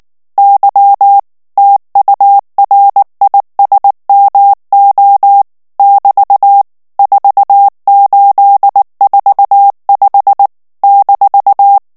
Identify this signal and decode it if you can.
Morse code, heard as a keyed tone, about 19 wpm, 800 Hz. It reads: Y TURISMO <BT> 4845 -
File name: ESPAGNOL TEXTE EN CLAIR 2 VITESSE 18 MOTS MINUTES